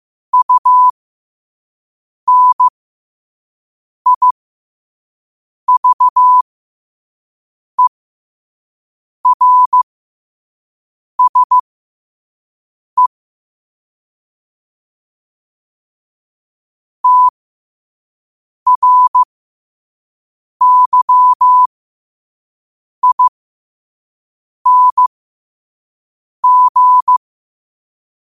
Morse code, heard as a keyed tone, about 15 wpm, spaced as for 5 wpm, 1000 Hz. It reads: UNIVERSE TRYING
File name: qotdfiles / 5wpm